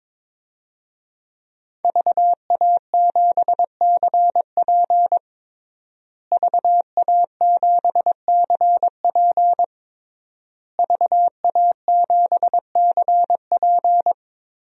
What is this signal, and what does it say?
Morse code, heard as a keyed tone, about 22 wpm, 700 Hz. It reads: VA7CP VA7CP VA7CP
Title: DXCC-100 Call Signs - 5 Characters - Repeated 3x - 22wpm